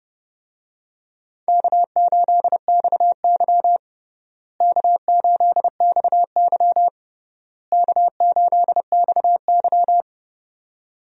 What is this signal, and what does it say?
Morse code, heard as a keyed tone, about 30 wpm, 700 Hz. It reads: K8XY K8XY K8XY
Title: USA Call Signs - 4 Characters - Repeated 3x - 30wpm